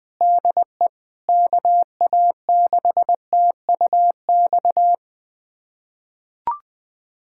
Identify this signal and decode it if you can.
Morse code, heard as a keyed tone, about 20 wpm, 700 Hz. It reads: DE KA6TUX
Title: USA Callsign DE Callsign 20wpm